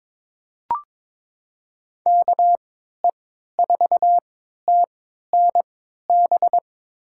Morse code, heard as a keyed tone, about 21 wpm, spaced as for 12 wpm, 700 Hz. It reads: KE4TNB
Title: USA Call Signs - 3x Character Spacing - 22wpm